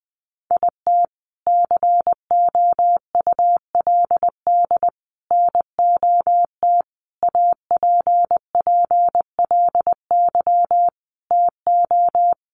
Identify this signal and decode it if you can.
Morse code, heard as a keyed tone, about 20 wpm, 700 Hz. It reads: IT COULD NOT APPLY TO
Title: Sentences from Top 1200 Words 20wpm